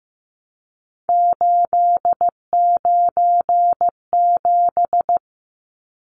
Morse code, heard as a keyed tone, about 15 wpm, 700 Hz. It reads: 897
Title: Numbers - 3-Digit 15wpm